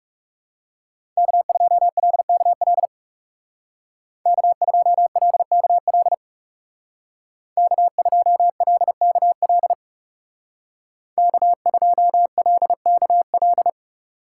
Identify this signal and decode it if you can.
Morse code, heard as a keyed tone, about 36 wpm, 700 Hz. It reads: K2LKL K2LKL K2LKL K2LKL
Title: Speed Racing - USA 1x3 Call Signs 30wpm